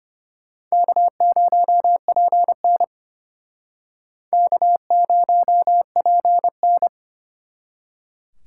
Morse code, heard as a keyed tone, about 34 wpm, 700 Hz. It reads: K0PN K0PN